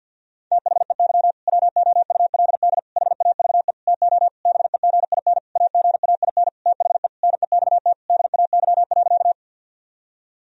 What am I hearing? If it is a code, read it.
Morse code, heard as a keyed tone, about 51 wpm, 700 Hz. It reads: THEY WOULD HAVE TO BEGIN AGAIN THE NEXT DAY.
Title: Sentences from Top 1000 Words - Review Part 1 - 50wpm